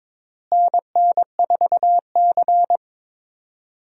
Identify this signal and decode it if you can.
Morse code, heard as a keyed tone, about 22 wpm, 700 Hz. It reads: NN4C
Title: USA Call Signs - 4 Characters - Repeated 3x - 22wpm